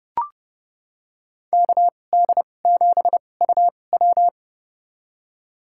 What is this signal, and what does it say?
Morse code, heard as a keyed tone, about 30 wpm, spaced as for 22 wpm, 700 Hz. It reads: KD7UW
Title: USA Call Signs - 5 Characters - 2x Character Spacing - 30wpm